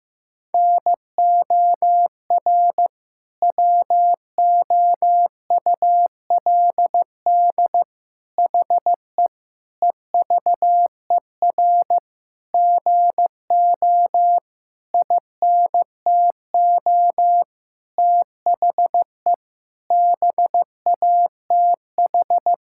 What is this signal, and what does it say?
Morse code, heard as a keyed tone, about 15 wpm, 700 Hz. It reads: NOR WOULD HE EVER GO INTO THE BATH
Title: Sentences from Top 1600 Words 15wpm